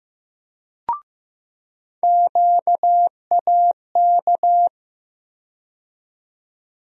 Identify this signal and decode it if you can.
Morse code, heard as a keyed tone, about 15 wpm, 700 Hz. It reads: QAK
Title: USA Call Sign Postfixes - 3 Letters 15wpm